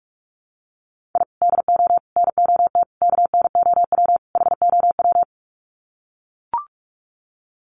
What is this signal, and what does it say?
Morse code, heard as a keyed tone, about 44 wpm, 700 Hz. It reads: I DO NOT KNOW HOW
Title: Sets of 5 Words 45wpm